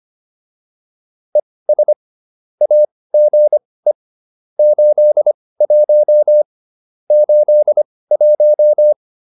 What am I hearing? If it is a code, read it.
Morse code, heard as a keyed tone, about 25 wpm, spaced as for 18 wpm, 600 Hz. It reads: ES AGE 81 81